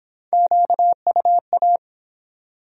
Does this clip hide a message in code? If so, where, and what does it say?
Yes: Morse code from 0.3 to 1.8 s, heard as a keyed tone, about 24 wpm, 700 Hz. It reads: QUA